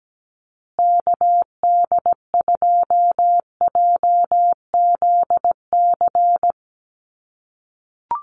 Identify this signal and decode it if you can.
Morse code, heard as a keyed tone, about 17 wpm, 700 Hz. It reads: KD2JZC
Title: USA Call Signs Repeated 2x 17wpm